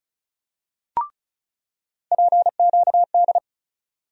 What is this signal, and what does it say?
Morse code, heard as a keyed tone, about 34 wpm, 700 Hz. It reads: PQD
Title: USA Call Sign Postfixes - 3 Letters 35wpm